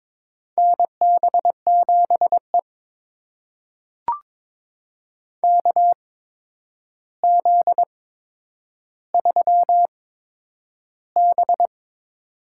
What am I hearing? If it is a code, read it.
Morse code, heard as a keyed tone, about 22 wpm, spaced as for 9 wpm, 700 Hz. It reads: NB7E KZ3B
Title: USA Call Signs - 4 Characters - 8x Character Spacing - 22wpm